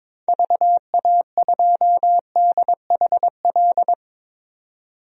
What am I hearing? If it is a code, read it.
Morse code, heard as a keyed tone, about 22 wpm, 700 Hz. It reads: VA2DHL
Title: DXCC-100 Call Signs - 6 Characters - 22wpm